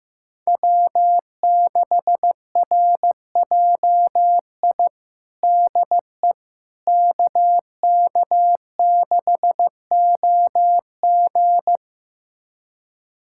Morse code, heard as a keyed tone, about 15 wpm, 700 Hz. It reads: W6RJI DE KK6OG
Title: USA Callsign DE Callsign - 5 Characters - 15wpm